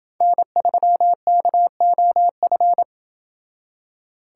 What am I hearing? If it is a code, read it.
Morse code, heard as a keyed tone, about 28 wpm, 700 Hz. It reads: N3KOF